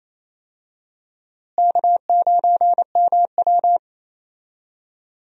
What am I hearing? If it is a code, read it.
Morse code, heard as a keyed tone, about 28 wpm, 700 Hz. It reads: K9MW
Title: USA Call Signs - 4 Characters - Repeated 2x - 28wpm